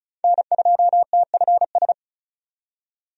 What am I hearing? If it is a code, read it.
Morse code, heard as a keyed tone, about 34 wpm, 700 Hz. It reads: N2TFS
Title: USA Call Signs - 5 Characters - 35wpm